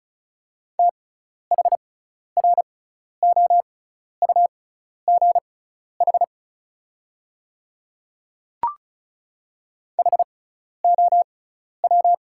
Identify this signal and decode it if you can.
Morse code, heard as a keyed tone, about 36 wpm, spaced as for 12 wpm, 690 Hz. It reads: THROUGH HOW
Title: 6x Character Spacing - Top 500 Words - 35wpm